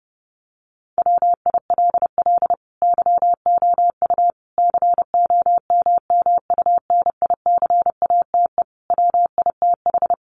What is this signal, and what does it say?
Morse code, heard as a keyed tone, about 30 wpm, 700 Hz. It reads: WILL YOU COMMUNICATE WITH
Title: Sentences from Top 1300 Words 30wpm